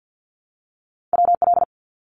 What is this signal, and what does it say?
Morse code, heard as a keyed tone, about 50 wpm, 700 Hz. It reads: WL
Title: Abbreviations 50wpm